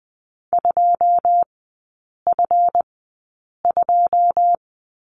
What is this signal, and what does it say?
Morse code, heard as a keyed tone, about 20 wpm, spaced as for 8 wpm, 700 Hz. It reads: 2F2
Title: 138 Lesson 23e - Letter f - Three Characters 20wpm